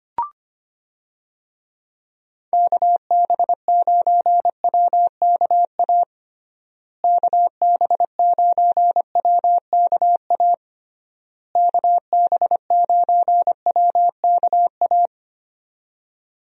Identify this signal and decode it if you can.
Morse code, heard as a keyed tone, about 25 wpm, 700 Hz. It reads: KB9WKA KB9WKA KB9WKA
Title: USA Call Signs - 6 Characters - Repeated 3x - 25wpm